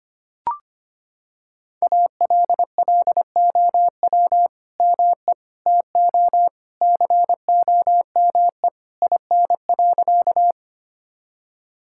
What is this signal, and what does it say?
Morse code, heard as a keyed tone, about 25 wpm, 700 Hz. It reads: ALLOW ME TO COME IN.
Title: Sentences from Top 1000 Words - Review Part 5 - 25wpm